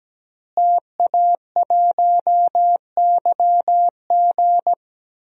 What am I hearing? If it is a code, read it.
Morse code, heard as a keyed tone, about 17 wpm, 700 Hz. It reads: TA1YG
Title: DXCC-100 Call Signs - 5 Characters - Repeated 3x - 17wpm